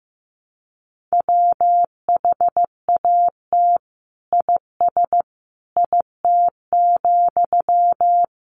Morse code, heard as a keyed tone, about 15 wpm, 700 Hz. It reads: WHAT IS IT,
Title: Sentences from Top 1200 Words 15wpm